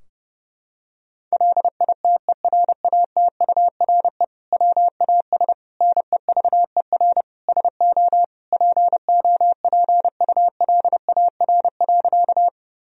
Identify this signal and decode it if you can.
Morse code, heard as a keyed tone, about 30 wpm, 700 Hz. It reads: LITERATURE WAS NEVER SO POPULAR.
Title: Sentences from Top 1000 Words - Review Part 3 - 30wpm